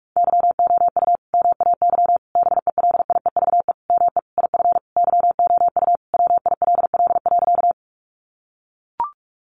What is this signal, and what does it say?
Morse code, heard as a keyed tone, about 44 wpm, 700 Hz. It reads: YOU MAY BELIEVE ME IF YOU WILL.
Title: Sentences from Top 1000 Words 45wpm